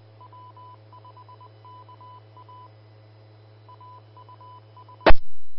Morse code, heard as a keyed tone, about 20 wpm, 1000 Hz. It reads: W5KA AUS
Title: Austin, TX Amateur Repeaters